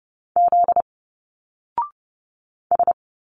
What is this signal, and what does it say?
Morse code, heard as a keyed tone, about 30 wpm, 700 Hz. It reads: Z S
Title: Single Letters 30wpm